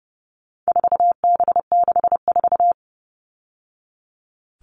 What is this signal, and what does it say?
Morse code, heard as a keyed tone, about 30 wpm, 700 Hz. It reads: 4B64
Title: All Cut Numbers - 4-Digit 30wpm